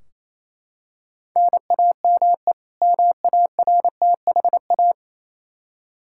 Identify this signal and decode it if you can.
Morse code, heard as a keyed tone, about 28 wpm, 700 Hz. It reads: NAME MARTHA
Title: Practice QSO - Exchange 1 - Beginning - 28wpm